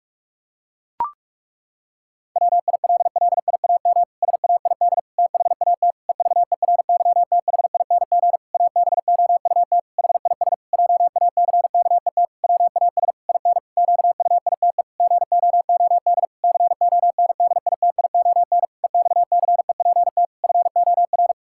Morse code, heard as a keyed tone, about 43 wpm, 700 Hz. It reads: WILLIAM SAID THAT EVERYTHING ABOUT HIS JACKET WAS IN QUITE GOOD CONDITION EXCEPT FOR
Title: Pangram Challenge v2 - 45wpm